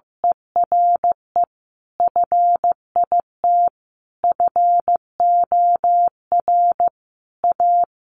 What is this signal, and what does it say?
Morse code, heard as a keyed tone, about 15 wpm, 700 Hz. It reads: ERE FIT FOR A
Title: Sentences from Top 800 Words 15wpm